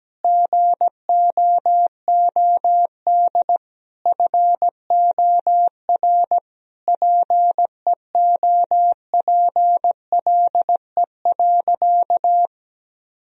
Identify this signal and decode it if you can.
Morse code, heard as a keyed tone, about 17 wpm, 700 Hz. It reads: GOOD FOR PEOPLE.
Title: Sentences from Top 1000 Words - Review Part 3 - 17wpm